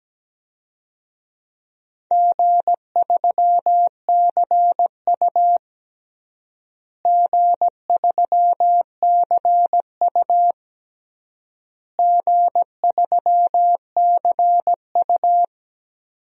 Morse code, heard as a keyed tone, about 17 wpm, 700 Hz. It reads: G3CU G3CU G3CU
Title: DXCC-100 Call Signs - 4 Characters - Repeated 3x - 17wpm